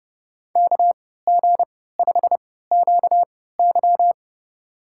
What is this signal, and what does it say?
Morse code, heard as a keyed tone, about 30 wpm, spaced as for 17 wpm, 700 Hz. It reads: KG5QY
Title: USA Call Signs - 5 Characters - 3x Character Spacing - 30wpm